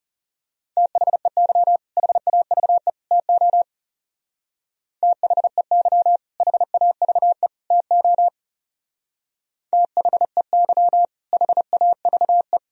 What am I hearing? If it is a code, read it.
Morse code, heard as a keyed tone, about 34 wpm, 700 Hz. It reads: THEY HAVE TO THEY HAVE TO THEY HAVE
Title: Speed-Racing - Sets of 3 Words 30wpm